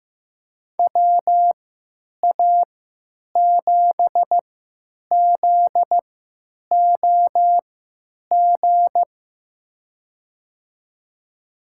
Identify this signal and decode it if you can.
Morse code, heard as a keyed tone, about 15 wpm, spaced as for 9 wpm, 700 Hz. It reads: WA7ZOG
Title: USA Call Signs - 6 Characters - 3x Character Spacing - 15wpm